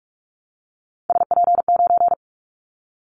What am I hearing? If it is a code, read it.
Morse code, heard as a keyed tone, about 46 wpm, 700 Hz. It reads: SP9
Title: Callsign Prefixes for Top 25 DXCC - 45wpm